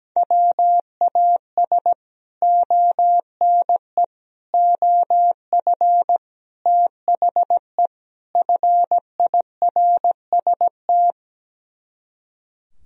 Morse code, heard as a keyed tone, about 17 wpm, 700 Hz. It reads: WAS ONE OF THE FIRST
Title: Speed-Racing - Sets of 5 Words 17wpm